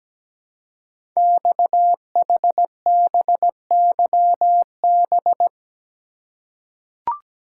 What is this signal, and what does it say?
Morse code, heard as a keyed tone, about 17 wpm, 700 Hz. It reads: XHBYB
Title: Code Groups - 5 Letters - 17wpm